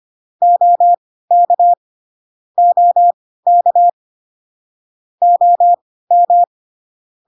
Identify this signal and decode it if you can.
Morse code, heard as a keyed tone, about 25 wpm, spaced as for 16 wpm, 700 Hz. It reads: OK OK OM